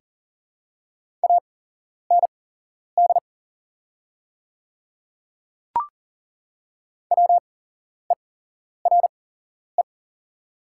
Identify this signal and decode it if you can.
Morse code, heard as a keyed tone, about 40 wpm, spaced as for 11 wpm, 700 Hz. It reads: AND WERE
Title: 8x Character Spacing - Top 100 Words - 40wpm